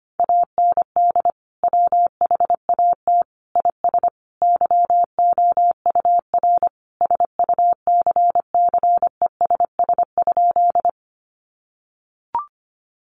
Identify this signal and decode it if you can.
Morse code, heard as a keyed tone, about 25 wpm, 700 Hz. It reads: AND WHAT IS YOUR SUCCESS?
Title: Sentences from Top 900 Words 25wpm